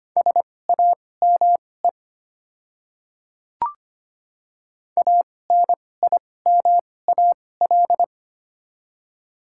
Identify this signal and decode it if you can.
Morse code, heard as a keyed tone, about 25 wpm, spaced as for 18 wpm, 700 Hz. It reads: SAME ANIMAL
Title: 2x Character Spacing - Top 300 Words - 25wpm